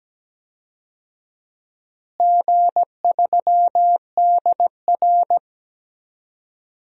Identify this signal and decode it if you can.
Morse code, heard as a keyed tone, about 17 wpm, 700 Hz. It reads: G3DR
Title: DXCC-100 Call Signs - 4 Characters - Repeated 3x - 17wpm